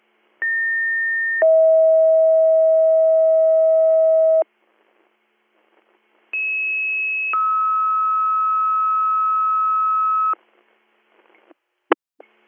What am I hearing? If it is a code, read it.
Two-tone page, heard as a one-second tone, then a three-second tone, about 1800 then 640 Hz, about 2600 then 1290 Hz. It is heard from 0.4 to 10.3 s.